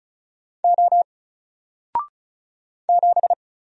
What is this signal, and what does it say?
Morse code, heard as a keyed tone, about 36 wpm, 700 Hz. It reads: O 7
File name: Single-Letter-Number-35wpm